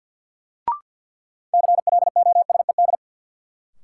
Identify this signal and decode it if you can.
Morse code, heard as a keyed tone, about 52 wpm, 700 Hz. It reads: CLOSED
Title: Sentences from Top 1400 Words - New Word Drill - 50wpm